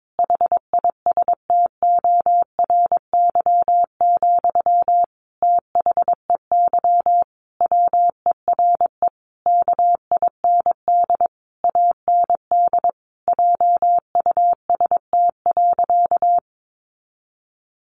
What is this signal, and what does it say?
Morse code, heard as a keyed tone, about 22 wpm, 700 Hz. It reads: HISTORY, THEY WERE KIND AND JUST.
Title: Sentences from Top 800 Words 22wpm